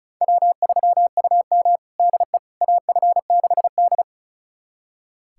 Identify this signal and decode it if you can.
Morse code, heard as a keyed tone, about 36 wpm, 700 Hz. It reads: W3UM DE AF6D
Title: USA Callsign DE Callsign - 4 Characters - 35wpm